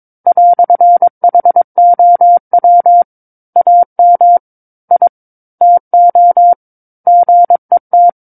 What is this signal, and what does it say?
Morse code, heard as a keyed tone, about 22 wpm, 700 Hz. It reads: "HOW AM I TO GET
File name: 06-Chapter-06 22 WPM